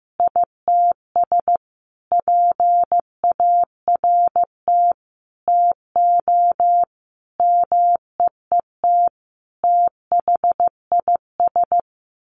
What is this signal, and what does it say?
Morse code, heard as a keyed tone, about 15 wpm, 700 Hz. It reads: ITS PART TO MEET THIS
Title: Sentences from Top 1100 Words 15wpm